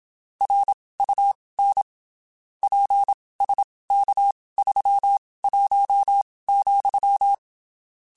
Morse code, heard as a keyed tone, about 27 wpm, spaced as for 19 wpm, 800 Hz. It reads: RUN PSK31,
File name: (spaced) / QSOs Spaced-mp3s / WA7QOX 2 @25 sp